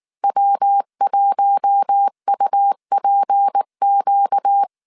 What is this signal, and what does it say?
Morse code, heard as a keyed tone, about 19 wpm, 800 Hz. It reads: W1UPQ
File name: CW / DET / 1 6 1500C S A 18wpm 2255